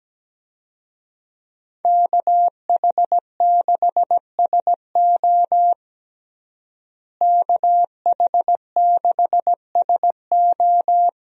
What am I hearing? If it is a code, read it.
Morse code, heard as a keyed tone, about 17 wpm, 700 Hz. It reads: KH6SO KH6SO
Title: USA Call Signs - 5 Characters - Repeated 3x - 17wpm